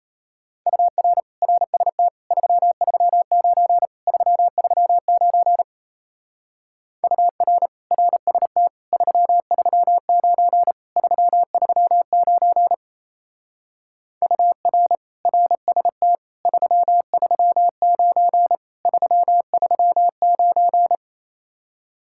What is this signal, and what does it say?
Morse code, heard as a keyed tone, about 33 wpm, 700 Hz. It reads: UR RST 339 339 UR RST 339 339 UR RST 339 339
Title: Speed-Racing Common QSO Phrases Encore - 28wpm